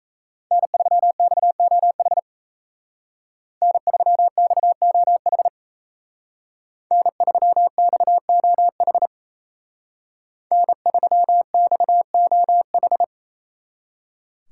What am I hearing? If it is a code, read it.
Morse code, heard as a keyed tone, about 33 wpm, 700 Hz. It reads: N3XOH N3XOH N3XOH N3XOH
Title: Speed Racing - USA 1x3 Call Signs 28wpm